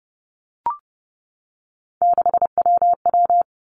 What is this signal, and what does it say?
Morse code, heard as a keyed tone, about 30 wpm, 700 Hz. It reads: 6WW
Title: Three Challenging Letter-Number Combinations 30wpm